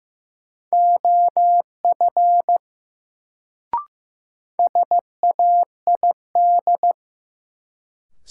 Morse code, heard as a keyed tone, about 15 wpm, 700 Hz. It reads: OF SAID
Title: Sentences from Top 100 Words - New Word Drill - 15wpm